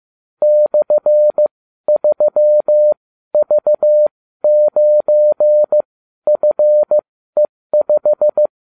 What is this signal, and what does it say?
Morse code, heard as a keyed tone, about 15 wpm, spaced as for 12 wpm, 600 Hz. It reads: /3V9FE5